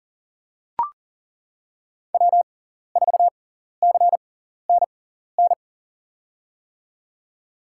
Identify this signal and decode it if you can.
Morse code, heard as a keyed tone, about 40 wpm, spaced as for 14 wpm, 700 Hz. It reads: W4CNN
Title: USA Call Signs - 5 Characters - 6x Character Spacing - 40wpm